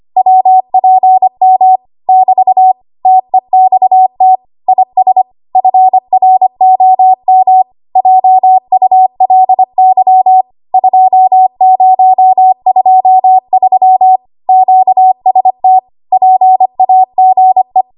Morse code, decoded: WPM <BT> TEXT IS FROM JULY 2023 QST PAGE